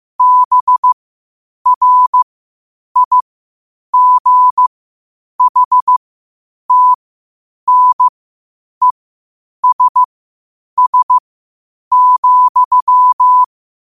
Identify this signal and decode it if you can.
Morse code, heard as a keyed tone, about 15 wpm, spaced as for 8 wpm, 1000 Hz. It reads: BRIGHTNESS,
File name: qotdfiles / 8wpm